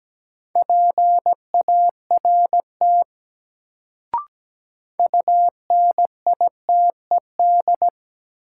Morse code, heard as a keyed tone, about 17 wpm, 700 Hz. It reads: PART UNITED
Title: Sets of 2 Words - New Word Drill - 17wpm